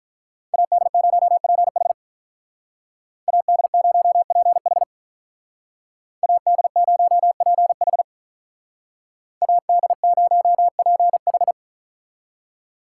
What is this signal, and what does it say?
Morse code, heard as a keyed tone, about 41 wpm, 700 Hz. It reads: AD0PH AD0PH AD0PH AD0PH
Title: Speed Racing - USA 2x2 Call Signs 35wpm